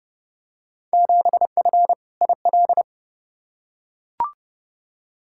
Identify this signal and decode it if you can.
Morse code, heard as a keyed tone, about 30 wpm, 700 Hz. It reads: 7F IL
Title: Field Day Exchanges Encore - 30wpm